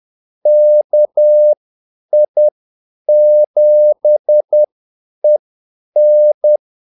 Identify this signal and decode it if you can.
Morse code, heard as a keyed tone, about 10 wpm, spaced as for 8 wpm, 600 Hz. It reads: KI7EN
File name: Callsigns-10-8wpm